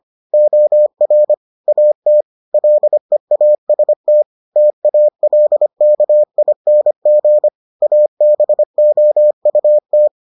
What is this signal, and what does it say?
Morse code, heard as a keyed tone, about 25 wpm, 600 Hz. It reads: OR AT LEAST TALKING ABOUT